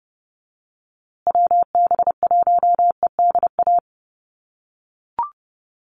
Morse code, heard as a keyed tone, about 30 wpm, 700 Hz. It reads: WB1EDA
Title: USA Call Signs Repeated 2x 30wpm